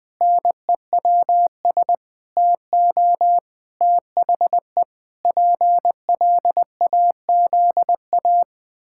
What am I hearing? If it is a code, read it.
Morse code, heard as a keyed tone, about 20 wpm, 700 Hz. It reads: NEWS TO THE PLAZA
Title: Pangram Challenge v2 - 20wpm